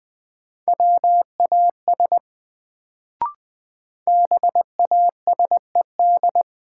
Morse code, heard as a keyed tone, about 20 wpm, 700 Hz. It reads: WAS BASED
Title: Sets of 2 Words - New Word Drill - 20wpm